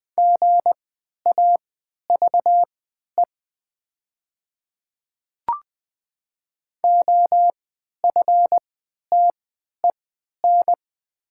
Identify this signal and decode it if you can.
Morse code, heard as a keyed tone, about 20 wpm, spaced as for 11 wpm, 700 Hz. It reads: GAVE OFTEN
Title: 3x Character Spacing - Top 500 Words - 20wpm